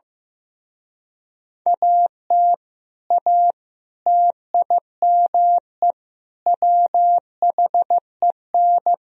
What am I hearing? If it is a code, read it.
Morse code, heard as a keyed tone, about 16 wpm, 700 Hz. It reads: AT A TIME WHEN
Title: Speed-Racing - Sets of 4 Words 15wpm